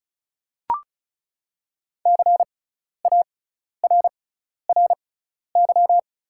Morse code, heard as a keyed tone, about 36 wpm, spaced as for 12 wpm, 700 Hz. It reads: CARRY
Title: 6x Character Spacing - Top 300 Words - 35wpm